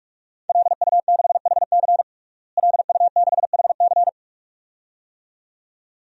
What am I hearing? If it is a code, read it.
Morse code, heard as a keyed tone, about 46 wpm, 700 Hz. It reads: LU6HC LU6HC
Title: DXCC-100 Call Signs - 5 Characters - Repeated 2x - 45wpm